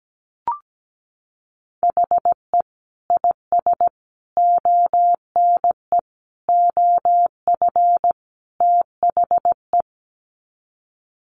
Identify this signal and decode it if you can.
Morse code, heard as a keyed tone, about 17 wpm, 700 Hz. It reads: HE IS ONE OF THE
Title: Sets of 5 Words Encore 17wpm